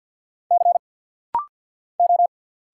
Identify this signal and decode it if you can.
Morse code, heard as a keyed tone, about 46 wpm, 700 Hz. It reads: C O
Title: Single Letter-Number Alt - Rapid-Fire 50wpm